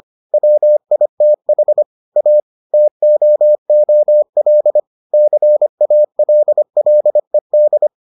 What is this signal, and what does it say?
Morse code, heard as a keyed tone, about 25 wpm, 600 Hz. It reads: WITH A TOOL CALLED